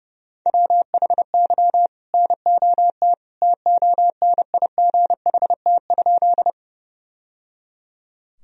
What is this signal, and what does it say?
Morse code, heard as a keyed tone, about 30 wpm, 700 Hz. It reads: WHY NOT TONIGHT?
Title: Sentences from Top 1000 Words - Review Part 3 - 30wpm